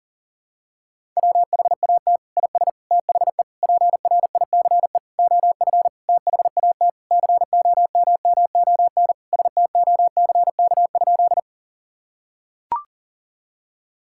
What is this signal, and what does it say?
Morse code, heard as a keyed tone, about 40 wpm, 700 Hz. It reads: WHAT IS THE PRICE OF THAT COMMON STOCK?
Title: Sentences from Top 1000 Words - Review Part 5 - 40wpm